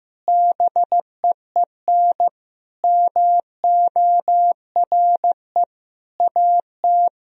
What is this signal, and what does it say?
Morse code, heard as a keyed tone, about 16 wpm, 700 Hz. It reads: BEEN MORE AT
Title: Sentences from Top 1000 Words - Review Part 1 - 15wpm